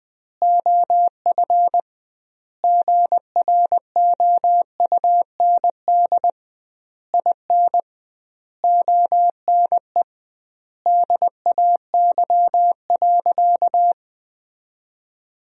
Morse code, heard as a keyed tone, about 20 wpm, 700 Hz. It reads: OF GROUND IN ONE DAY.
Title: 2x Word-Spacing - Sentences from Top 500 Words - 20wpm